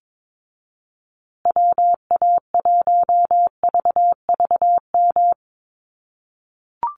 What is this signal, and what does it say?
Morse code, heard as a keyed tone, about 22 wpm, 700 Hz. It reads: WA1VVM
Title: USA Call Signs Repeated 2x 22wpm